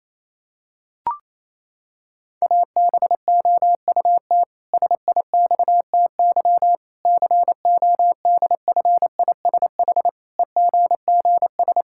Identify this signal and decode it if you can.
Morse code, heard as a keyed tone, about 27 wpm, 700 Hz. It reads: ABOUT SIXTY CODFISH EGGS
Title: Pangram Challenge v2 - 28wpm